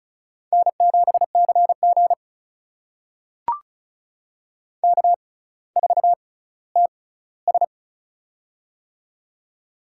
Morse code, heard as a keyed tone, about 36 wpm, spaced as for 18 wpm, 700 Hz. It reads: N7CG K4TS
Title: USA Call Signs - 4 Characters - 6x Character Spacing - 35wpm